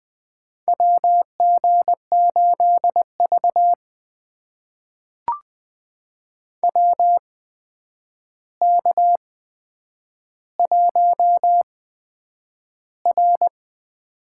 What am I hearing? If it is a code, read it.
Morse code, heard as a keyed tone, about 20 wpm, spaced as for 9 wpm, 700 Hz. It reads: WG8V WK1R
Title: USA Call Signs - 4 Characters - 8x Character Spacing - 20wpm